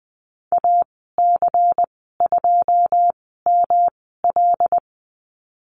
Morse code, heard as a keyed tone, about 19 wpm, spaced as for 14 wpm, 700 Hz. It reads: AC2ML
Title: 213 Lesson 34g - Number 8 - Call Signs 20wpm